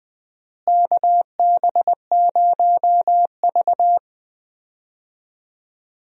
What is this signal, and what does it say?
Morse code, heard as a keyed tone, about 20 wpm, 700 Hz. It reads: KB0V
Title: USA Call Signs - 4 Characters - Repeated 3x - 20wpm